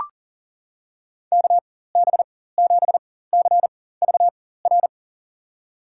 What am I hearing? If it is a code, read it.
Morse code, heard as a keyed tone, about 40 wpm, spaced as for 19 wpm, 700 Hz. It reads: KB7CVR
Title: USA Call Signs - 6 Characters - 4x Character Spacing - 40wpm